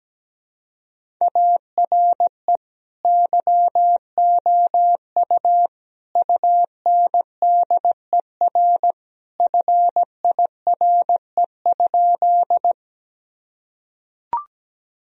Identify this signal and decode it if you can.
Morse code, heard as a keyed tone, about 17 wpm, 700 Hz. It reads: ARE YOU UNDER FIRE?
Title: Sentences from Top 1000 Words - Review Part 1 - 17wpm